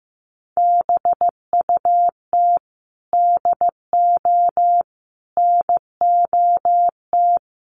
Morse code, heard as a keyed tone, about 15 wpm, 700 Hz. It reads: BUT DO NOT
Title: Sentences from Top 1100 Words 15wpm